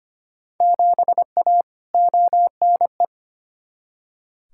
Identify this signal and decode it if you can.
Morse code, heard as a keyed tone, about 25 wpm, 700 Hz. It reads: 7A ONE
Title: Field Day Exchanges Encore - 25wpm